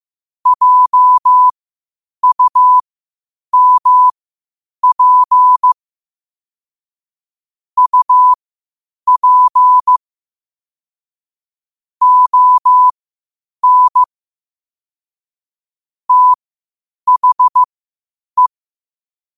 Morse code, heard as a keyed tone, about 15 wpm, spaced as for 8 wpm, 1000 Hz. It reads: JUMP UP ON THE